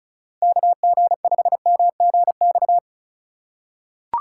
Morse code, heard as a keyed tone, about 34 wpm, 700 Hz. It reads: KG5MGX
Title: USA Call Signs - 6 Characters - 2x Character Spacing - 35wpm